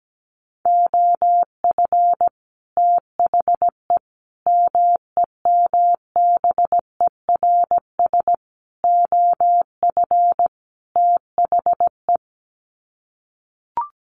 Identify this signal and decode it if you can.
Morse code, heard as a keyed tone, about 17 wpm, 700 Hz. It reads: OF THE MEMBERS OF THE
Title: Sets of 5 Words Encore2 17wpm